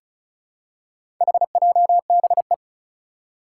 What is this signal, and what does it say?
Morse code, heard as a keyed tone, about 36 wpm, 700 Hz. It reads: HJBE
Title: Four Challenging Letter-Number Combinations 35wpm